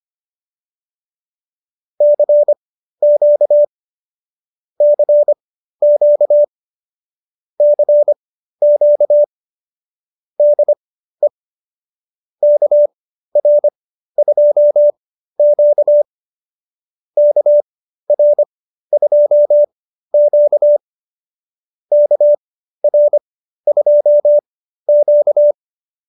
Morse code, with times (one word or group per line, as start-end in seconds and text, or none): 2.0-3.6 CQ
4.8-6.4 CQ
7.6-9.2 CQ
10.4-11.3 DE
12.4-16.0 KR2Q
17.2-20.8 KR2Q
21.9-25.5 KR2Q